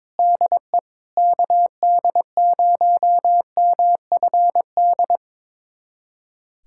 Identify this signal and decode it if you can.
Morse code, heard as a keyed tone, about 22 wpm, 700 Hz. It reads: DE KD0MFD
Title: USA Callsign DE Callsign 22wpm